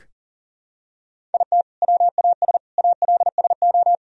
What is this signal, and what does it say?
Morse code, heard as a keyed tone, about 40 wpm, 700 Hz. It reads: IT WAS ALSO
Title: Sentences from Top 1000 Words - Review Part 1 - 40wpm